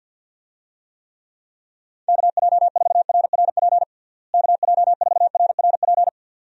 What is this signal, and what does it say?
Morse code, heard as a keyed tone, about 50 wpm, 700 Hz. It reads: KJ4RRP KJ4RRP
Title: USA Call Signs - 6 Characters - Repeated 2x - 50wpm